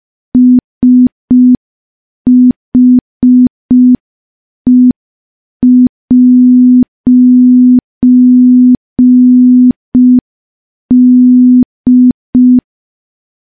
Morse code, heard as a keyed tone, about 5 wpm, 250 Hz. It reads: SHE'D
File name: morse / morse-training-05WPM